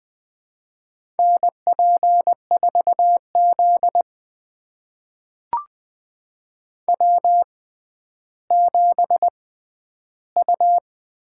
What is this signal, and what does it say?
Morse code, heard as a keyed tone, about 20 wpm, spaced as for 11 wpm, 700 Hz. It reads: NP4Z W7U
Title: USA Call Signs - 4 Characters - 6x Character Spacing - 20wpm